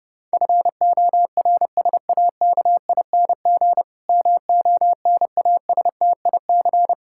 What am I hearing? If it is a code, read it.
Morse code, heard as a keyed tone, about 30 wpm, 700 Hz. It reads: FORSAKING MONASTIC
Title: Pangram Challenge v2 - 30wpm